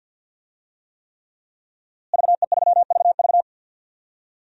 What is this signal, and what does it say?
Morse code, heard as a keyed tone, about 53 wpm, 700 Hz. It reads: VE3VV
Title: DXCC-100 Call Signs - 5 Characters - Repeated 3x - 50wpm